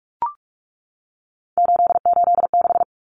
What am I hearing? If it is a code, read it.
Morse code, heard as a keyed tone, about 44 wpm, 700 Hz. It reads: 886
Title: Numbers - 3-Digit 45wpm